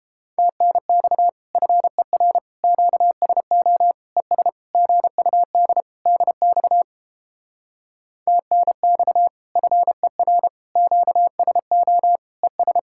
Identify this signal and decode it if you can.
Morse code, heard as a keyed tone, about 29 wpm, 700 Hz. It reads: TNX FER QSO ES GUD DX TNX FER QSO ES